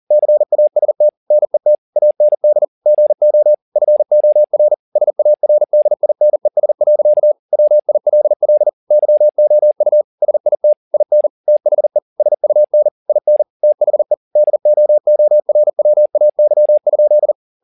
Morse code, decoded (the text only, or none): CAST NET AND GO FOR SARDINES. WILL YOU SIT IN THE SUN IN THE DOORWAY?